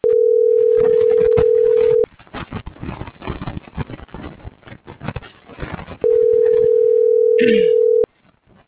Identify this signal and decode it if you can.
Ringback tone (North America), 440 + 480 Hz.